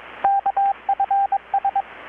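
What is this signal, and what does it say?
Morse code, heard as a keyed tone, about 23 wpm, 790 Hz, with radio noise. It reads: KFS
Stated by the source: Morse from KPH's Night of Nights XX maritime radiotelegraphy event, received in Chiapas, Mexico about 3,800 km away on a Kenwood R-600 with a grounded 10 m long-wire antenna.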